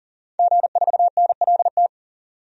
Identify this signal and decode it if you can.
Morse code, heard as a keyed tone, about 40 wpm, 700 Hz. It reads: G4NLT
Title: DXCC-100 Call Signs - 5 Characters - 40wpm